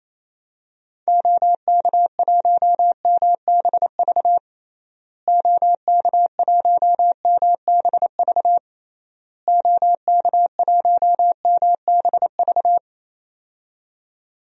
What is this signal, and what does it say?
Morse code, heard as a keyed tone, about 28 wpm, 700 Hz. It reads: OK1MBV OK1MBV OK1MBV
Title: DXCC-100 Call Signs - 6 Characters - Repeated 3x - 28wpm